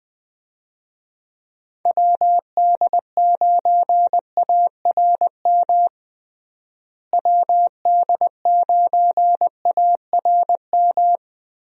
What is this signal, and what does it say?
Morse code, heard as a keyed tone, about 20 wpm, 700 Hz. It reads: WD9ARM WD9ARM
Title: USA Call Signs - 6 Characters - Repeated 2x - 20wpm